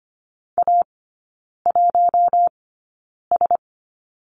Morse code, heard as a keyed tone, about 25 wpm, spaced as for 9 wpm, 690 Hz. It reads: A1S
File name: Lesson 14e - Number 2 - Three Characters 25wpm